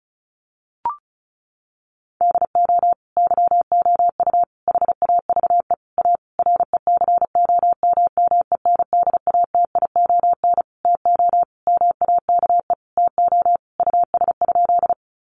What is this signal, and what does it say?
Morse code, heard as a keyed tone, about 34 wpm, 700 Hz. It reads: DO YOU HAVE A RECOMMENDATION TO MAKE TO US?
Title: Sentences from Top 1400 Words 35wpm